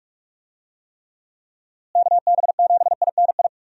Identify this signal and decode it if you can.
Morse code, heard as a keyed tone, about 45 wpm, 700 Hz. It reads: KB7INI
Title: USA Call Signs - 6 Characters - Repeated 2x - 45wpm